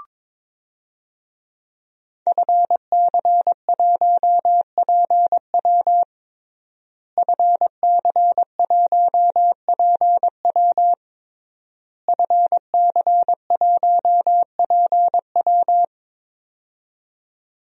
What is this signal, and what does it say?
Morse code, heard as a keyed tone, about 22 wpm, 700 Hz. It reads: FC1PW FC1PW FC1PW
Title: DXCC-100 Call Signs - 5 Characters - Repeated 3x - 22wpm